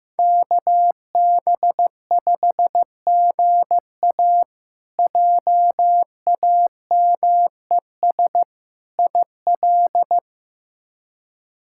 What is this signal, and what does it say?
Morse code, heard as a keyed tone, about 15 wpm, 700 Hz. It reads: KB5GA JAMES IL
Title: K1USN SST QSO Practice - 15wpm